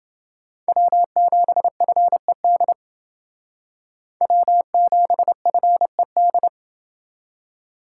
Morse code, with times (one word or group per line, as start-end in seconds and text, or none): 0.7-2.7 W7FED
4.2-6.5 W7FED